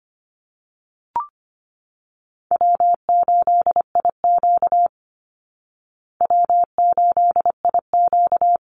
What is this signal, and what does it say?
Morse code, heard as a keyed tone, about 25 wpm, 700 Hz. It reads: W8IQ W8IQ
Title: Call Signs Repeated 3x - 25wpm